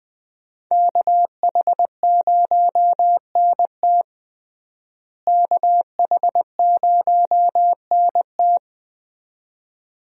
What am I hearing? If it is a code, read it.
Morse code, heard as a keyed tone, about 20 wpm, 700 Hz. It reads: KH0NT KH0NT
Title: USA Call Signs - 5 Characters - Repeated 2x - 20wpm